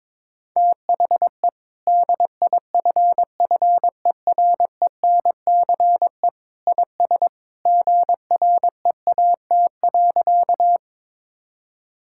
Morse code, heard as a keyed tone, about 21 wpm, 700 Hz. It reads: THE DIFFERENCE IS GREAT.
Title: Sentences from Top 1000 Words - Review Part 5 - 22wpm